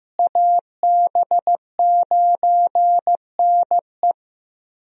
Morse code, heard as a keyed tone, about 16 wpm, 700 Hz. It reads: AB9NE
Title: USA Call Signs - 5 Characters - Repeated 3x - 15wpm